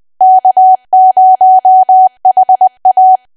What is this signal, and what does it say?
Morse code, heard as a keyed tone, about 20 wpm, 750 Hz. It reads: K0HA
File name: cwa / W1AW-Level3 / 20WPM